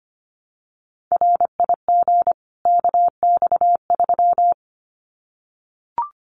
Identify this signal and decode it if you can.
Morse code, heard as a keyed tone, about 25 wpm, 700 Hz. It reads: RIG KX3
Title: Common QSO Phrases 25wpm